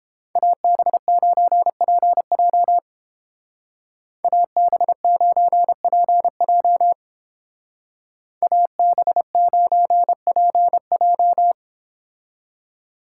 Morse code, heard as a keyed tone, about 30 wpm, 700 Hz. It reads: AB9PJ AB9PJ AB9PJ